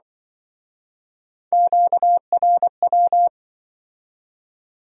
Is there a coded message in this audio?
Yes: Morse code from 1.5 to 3.3 s, heard as a keyed tone, about 25 wpm, 700 Hz. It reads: QRW